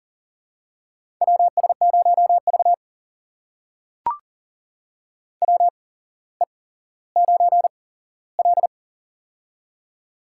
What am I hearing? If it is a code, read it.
Morse code, heard as a keyed tone, about 40 wpm, spaced as for 17 wpm, 700 Hz. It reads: WS0V WE9L
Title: USA Call Signs - 4 Characters - 8x Character Spacing - 40wpm